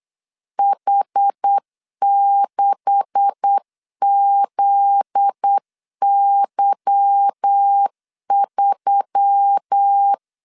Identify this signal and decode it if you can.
Morse code, heard as a keyed tone, about 8 wpm, 800 Hz. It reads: H6ZY3
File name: LEZIONI / 10 SIGNS / 10D COMA 8wpm 3749